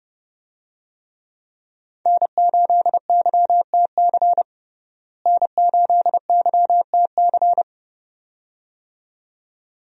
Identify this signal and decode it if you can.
Morse code, heard as a keyed tone, about 30 wpm, 700 Hz. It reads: N8YTC N8YTC
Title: USA Call Signs - 5 Characters - Repeated 2x - 30wpm